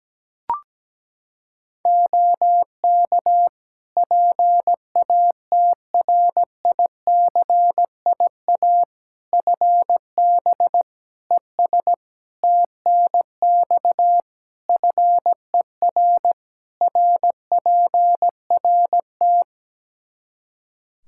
Morse code, heard as a keyed tone, about 17 wpm, 700 Hz. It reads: OK PATRICIA FB ES TNX FER RPRT
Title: Practice QSO - Exchange 2 - Info - 17wpm